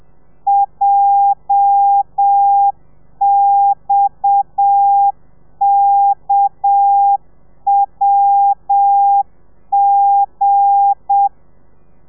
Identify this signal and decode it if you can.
Morse code, heard as a keyed tone, about 7 wpm, 800 Hz. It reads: JXKWG
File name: telegrafia / i2tfj / Lez8 30